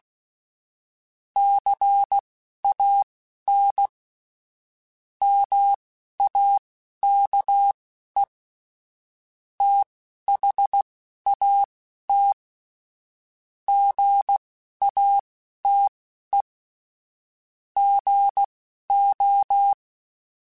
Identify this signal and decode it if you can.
Morse code, heard as a keyed tone, about 16 wpm, spaced as for 11 wpm, 800 Hz. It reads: CAN MAKE THAT GATE GO
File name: Technical Subjects / Technical Subjects-mp3s / OR Gate @10